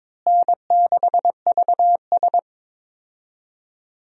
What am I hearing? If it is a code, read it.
Morse code, heard as a keyed tone, about 22 wpm, 700 Hz. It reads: N6VS
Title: USA Call Signs - 4 Characters - Repeated 3x - 22wpm